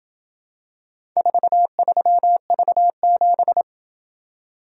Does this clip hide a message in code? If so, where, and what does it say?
Yes: Morse code from 1.2 to 3.6 s, heard as a keyed tone, about 27 wpm, 700 Hz. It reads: 43V7